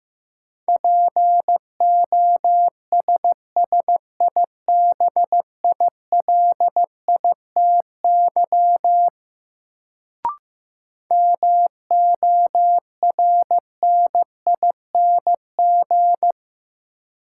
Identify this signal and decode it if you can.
Morse code, heard as a keyed tone, about 15 wpm, 700 Hz. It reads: POSSIBILITY MORNING
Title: Sets of 3 Words - New Word Drill - 15wpm